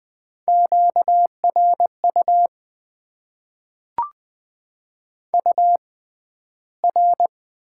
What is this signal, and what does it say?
Morse code, heard as a keyed tone, about 20 wpm, spaced as for 12 wpm, 700 Hz. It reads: QRU UR
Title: Top 90 QSO Elements - 6x Character Spacing - 20wpm